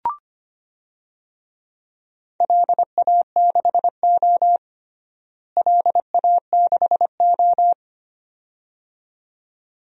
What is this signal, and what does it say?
Morse code, heard as a keyed tone, about 25 wpm, 700 Hz. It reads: LA6O LA6O
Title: DXCC-100 Call Signs - 4 Characters - Repeated 2x - 25wpm